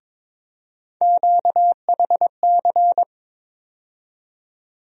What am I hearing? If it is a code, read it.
Morse code, heard as a keyed tone, about 22 wpm, 700 Hz. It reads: QHC
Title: USA Call Sign Postfixes - 3 Letters 22wpm